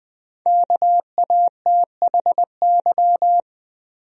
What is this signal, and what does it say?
Morse code, heard as a keyed tone, about 20 wpm, 700 Hz. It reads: KATHY
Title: USA Common Names - 20wpm